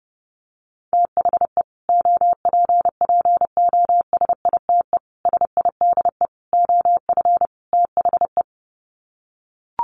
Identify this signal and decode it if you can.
Morse code, heard as a keyed tone, about 30 wpm, 700 Hz. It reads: THE OPPOSITE SIDE OF THE
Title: Sets of 5 Words Encore 30wpm